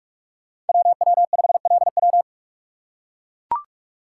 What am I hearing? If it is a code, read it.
Morse code, heard as a keyed tone, about 43 wpm, 700 Hz. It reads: WW5LW
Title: USA Call Signs - 2x Character Spacing - 45wpm